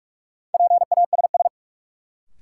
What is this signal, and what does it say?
Morse code, heard as a keyed tone, about 46 wpm, 700 Hz. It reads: PASS
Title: Sentences from Top 400 Words - New Word Drill - 45wpm